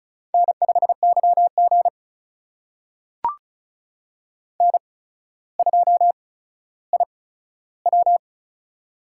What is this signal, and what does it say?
Morse code, heard as a keyed tone, about 36 wpm, spaced as for 15 wpm, 700 Hz. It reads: N5YG N2IW
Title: USA Call Signs - 4 Characters - 8x Character Spacing - 35wpm